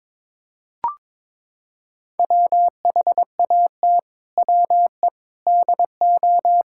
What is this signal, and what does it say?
Morse code, heard as a keyed tone, about 22 wpm, 700 Hz. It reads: WHAT WE DO
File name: practice / 22wpm / Sets of 7 Words - 22wpm